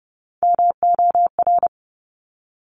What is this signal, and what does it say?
Morse code, heard as a keyed tone, about 30 wpm, 700 Hz. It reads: MOR
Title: Top Three Letter Combinations in Words 30wpm